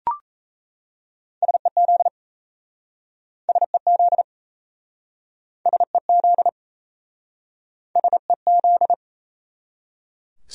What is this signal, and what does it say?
Morse code, heard as a keyed tone, about 38 wpm, 700 Hz. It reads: SEZ SEZ SEZ SEZ